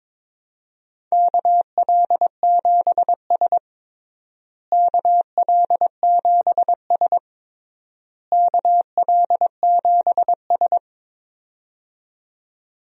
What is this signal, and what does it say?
Morse code, heard as a keyed tone, about 22 wpm, 700 Hz. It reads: KL7S KL7S KL7S
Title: DXCC-100 Call Signs - 4 Characters - Repeated 3x - 22wpm